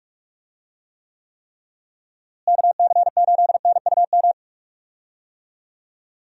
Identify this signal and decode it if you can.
Morse code, heard as a keyed tone, about 42 wpm, 700 Hz. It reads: KC8NUM
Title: USA Call Signs - 6 Characters - Repeated 3x - 45wpm